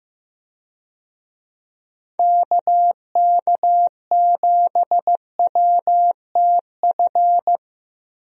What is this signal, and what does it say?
Morse code, heard as a keyed tone, about 15 wpm, 700 Hz. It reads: KK7WTF
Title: USA Call Signs - 6 Characters - Repeated 2x - 15wpm